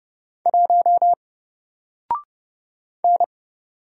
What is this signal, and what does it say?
Morse code, heard as a keyed tone, about 30 wpm, 700 Hz. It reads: 1 N
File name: Single-Letter-Number-30wpm